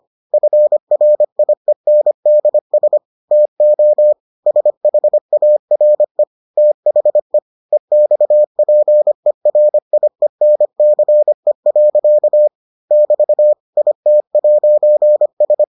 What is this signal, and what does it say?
Morse code, heard as a keyed tone, about 25 wpm, 600 Hz. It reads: FRIENDS TO SHARE THE EXPERIENCE. <BT> IT'S